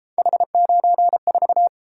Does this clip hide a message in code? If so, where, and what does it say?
Yes: Morse code from 0.2 to 1.7 s, heard as a keyed tone, about 30 wpm, 700 Hz. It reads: H94